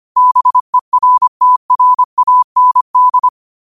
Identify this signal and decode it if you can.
Morse code, heard as a keyed tone, about 25 wpm, 1000 Hz. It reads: BERTRAND
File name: 25wpm